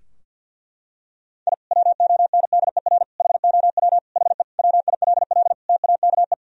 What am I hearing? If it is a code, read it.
Morse code, heard as a keyed tone, about 46 wpm, 700 Hz. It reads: I WONDER HOW HE WILL TAKE
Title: Sentences from Top 1000 Words - Review Part 3 - 50wpm